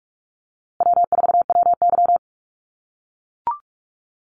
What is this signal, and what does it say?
Morse code, heard as a keyed tone, about 44 wpm, 700 Hz. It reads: W4WY
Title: USA Call Signs Repeated 3x 45wpm